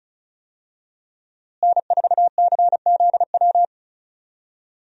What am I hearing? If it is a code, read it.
Morse code, heard as a keyed tone, about 36 wpm, 700 Hz. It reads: N4CZW
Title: USA Call Signs - 5 Characters - Repeated 2x - 35wpm